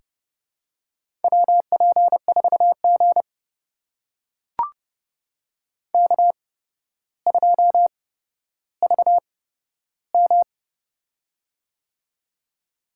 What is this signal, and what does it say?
Morse code, heard as a keyed tone, about 31 wpm, spaced as for 13 wpm, 700 Hz. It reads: WP4G K2VM
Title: USA Call Signs - 4 Characters - 8x Character Spacing - 30wpm